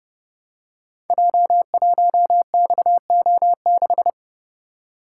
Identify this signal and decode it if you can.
Morse code, heard as a keyed tone, about 30 wpm, 700 Hz. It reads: J1XO6
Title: Code Groups - 5 Letter:Numbers - 30wpm